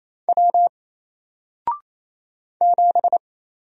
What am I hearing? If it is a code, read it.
Morse code, heard as a keyed tone, about 27 wpm, 700 Hz. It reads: W 7
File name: Single-Letter-Number-28wpm